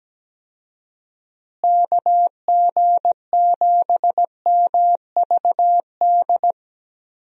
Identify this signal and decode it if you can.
Morse code, heard as a keyed tone, about 17 wpm, 700 Hz. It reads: KG7MVD
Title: USA Call Signs - 6 Characters - Repeated 2x - 17wpm